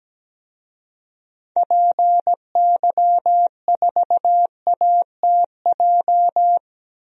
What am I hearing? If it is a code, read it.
Morse code, heard as a keyed tone, about 17 wpm, 700 Hz. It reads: PY4ATJ
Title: DXCC-100 Call Signs - 6 Characters - 17wpm